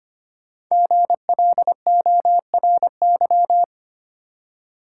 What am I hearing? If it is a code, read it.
Morse code, heard as a keyed tone, about 25 wpm, 700 Hz. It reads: GLORY
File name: 5 Letter Words - First 3 Letters Given - 25wpm